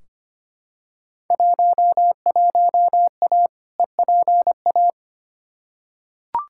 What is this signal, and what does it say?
Morse code, heard as a keyed tone, about 25 wpm, 700 Hz. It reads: 11A EPA
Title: Field Day Exchanges Encore - 25wpm